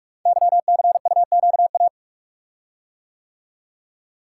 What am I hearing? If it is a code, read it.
Morse code, heard as a keyed tone, about 44 wpm, 700 Hz. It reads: YCUQA
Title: Code Groups - 5 Letters - 45wpm